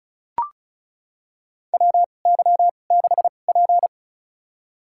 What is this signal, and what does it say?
Morse code, heard as a keyed tone, about 36 wpm, spaced as for 25 wpm, 700 Hz. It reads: WY6P
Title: USA Call Signs - 4 Characters - 2x Character Spacing - 35wpm